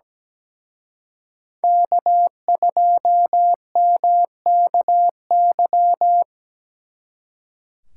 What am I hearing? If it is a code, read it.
Morse code, heard as a keyed tone, about 20 wpm, 700 Hz. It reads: K2MKY
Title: Speed Racing - USA 1x3 Call Signs 17wpm